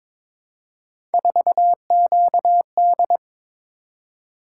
Morse code, heard as a keyed tone, about 22 wpm, 700 Hz. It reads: 4QD
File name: practice / 22wpm / Three Letter-Number Combinations - 22wpm